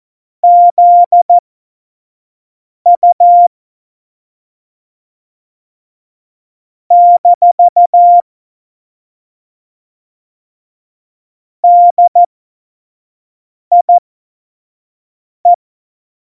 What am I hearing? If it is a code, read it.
Morse code, heard as a keyed tone, about 14 wpm, spaced as for 5 wpm, 700 Hz. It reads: ZU - DIE
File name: Morsekurs / Witze / Witz 11. Juni 2019, 25 F70